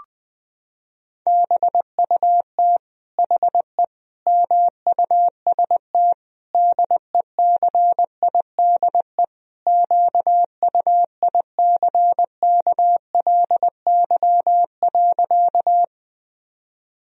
Morse code, heard as a keyed tone, about 20 wpm, 700 Hz. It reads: BUT HE MUST DECIDE QUICKLY.
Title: Sentences from Top 1000 Words - Review Part 3 - 20wpm